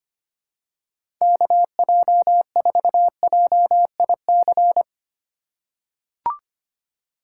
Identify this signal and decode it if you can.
Morse code, heard as a keyed tone, about 25 wpm, 700 Hz. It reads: KJ4JIC
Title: USA Call Signs - 4x Character Spacing - 25wpm